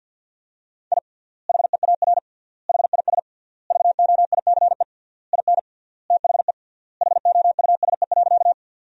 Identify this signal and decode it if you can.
Morse code, heard as a keyed tone, about 48 wpm, 700 Hz. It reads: I HEAR HIS VOICE IN THE HOUSE.
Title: 3x Word-Spacing - Sentences from Top 500 Words - 50wpm